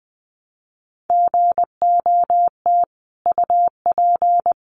Morse code, heard as a keyed tone, about 20 wpm, 700 Hz. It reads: GOT UP
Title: Sentences from Top 900 Words 20wpm